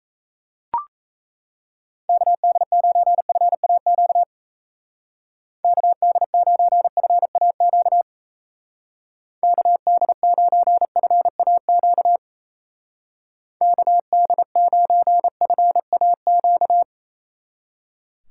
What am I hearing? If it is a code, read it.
Morse code, heard as a keyed tone, about 34 wpm, 700 Hz. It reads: KD9FAQ KD9FAQ KD9FAQ KD9FAQ
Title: Speed Racing - USA Call Signs 28wpm